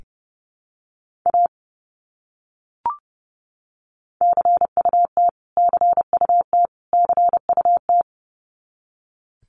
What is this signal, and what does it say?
Morse code, heard as a keyed tone, about 30 wpm, 700 Hz. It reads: A CUT CUT CUT
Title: Top 300 Words Repeated 30wpm